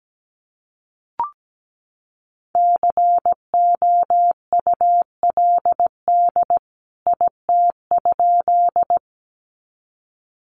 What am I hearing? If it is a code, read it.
Morse code, heard as a keyed tone, about 17 wpm, 700 Hz. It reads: COULD IT?
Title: A Sound Of Thunder 17wpm